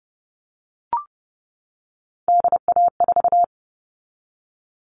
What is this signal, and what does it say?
Morse code, heard as a keyed tone, about 30 wpm, 700 Hz. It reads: DA4
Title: All Cut Numbers - 3-Digit 30wpm